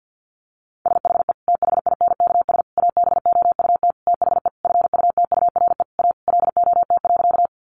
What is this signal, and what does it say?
Morse code, heard as a keyed tone, about 50 wpm, 700 Hz. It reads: SHE THINKS ABOUT THE FUTURE A LOT.
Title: Easy Phrases 50wpm